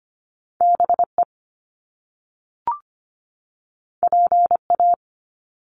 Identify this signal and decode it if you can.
Morse code, heard as a keyed tone, about 25 wpm, 700 Hz. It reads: BE PA
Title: Top Two Letter Combinations in Words 25wpm